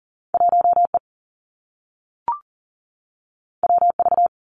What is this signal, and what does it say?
Morse code, heard as a keyed tone, about 41 wpm, 700 Hz. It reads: 1E WV
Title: Two Challenging Letter-Number Combinations 40wpm